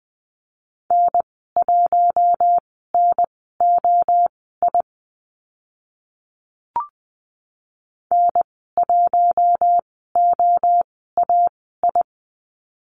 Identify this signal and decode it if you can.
Morse code, heard as a keyed tone, about 20 wpm, spaced as for 14 wpm, 700 Hz. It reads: N1NOI N1OAI
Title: 048 Lesson 9g - Number 4 - Call Signs 20wpm